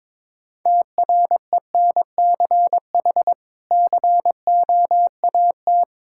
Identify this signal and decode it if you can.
Morse code, heard as a keyed tone, about 22 wpm, 700 Hz. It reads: TRENCH COAT
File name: Difficult Words with Hints - 22wpm